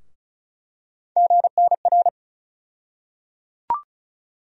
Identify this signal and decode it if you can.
Morse code, heard as a keyed tone, about 36 wpm, 700 Hz. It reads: GNR
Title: USA Call Sign Postfixes - 3 Letters 35wpm